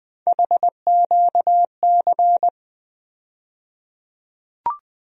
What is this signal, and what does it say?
Morse code, heard as a keyed tone, about 21 wpm, 700 Hz. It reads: HQC
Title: USA Call Sign Postfixes - 3 Letters 20wpm